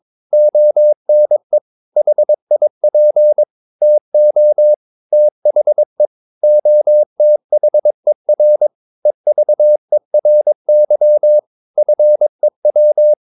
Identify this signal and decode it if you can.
Morse code, heard as a keyed tone, about 22 wpm, 600 Hz. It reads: ONE HIP TO THE OTHER EVERY FEW
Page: Romanen Never af Ken Follet på engelsk.